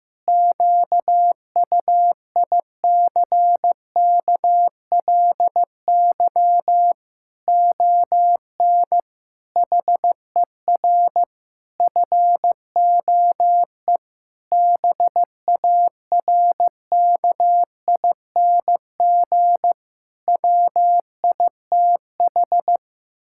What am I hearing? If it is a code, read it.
Morse code, heard as a keyed tone, about 15 wpm, 700 Hz. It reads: QUICKLY ON HER FOE BARKING WITH
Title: Pangram Challenge v2 - 15wpm